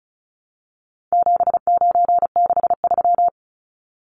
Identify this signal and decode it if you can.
Morse code, heard as a keyed tone, about 34 wpm, 700 Hz. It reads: 7963
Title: Numbers - 4-Digit 35wpm